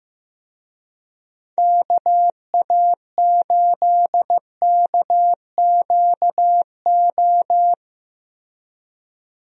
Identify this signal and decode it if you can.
Morse code, heard as a keyed tone, about 15 wpm, 700 Hz. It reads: KA8KQO
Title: USA Call Signs - 6 Characters - 15wpm